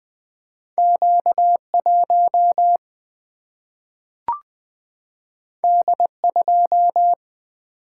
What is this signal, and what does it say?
Morse code, heard as a keyed tone, about 20 wpm, 700 Hz. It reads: Q1 D2
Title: Single-Letter Single-Number 20wpm